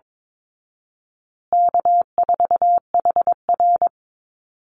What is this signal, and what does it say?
Morse code, heard as a keyed tone, about 22 wpm, 700 Hz. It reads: K4HR
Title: Easy Call Signs Repeated 3x - 22wpm